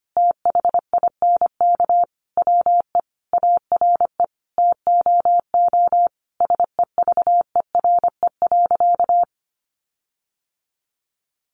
Morse code, heard as a keyed tone, about 25 wpm, 700 Hz. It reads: THINK WE ARE TOO SEVERE.
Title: Sentences from Top 1400 Words 25wpm